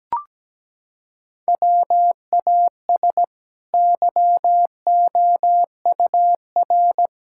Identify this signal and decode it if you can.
Morse code, heard as a keyed tone, about 17 wpm, 700 Hz. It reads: WAS YOUR
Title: Sentences from Top 1000 Words - Review Part 5 - 17wpm